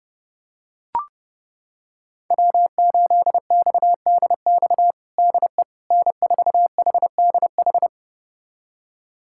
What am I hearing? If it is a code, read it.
Morse code, heard as a keyed tone, about 30 wpm, 700 Hz. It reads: W8XDX DE N4HDH
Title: USA Callsign DE Callsign - 5 Characters - 30wpm